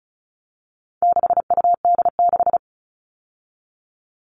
Morse code, heard as a keyed tone, about 34 wpm, 700 Hz. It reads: 6UD6
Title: All Cut Numbers - 4-Digit 35wpm